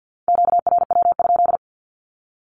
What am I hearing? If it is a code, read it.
Morse code, heard as a keyed tone, about 49 wpm, 700 Hz. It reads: QRW?
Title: Q-Codes 50wpm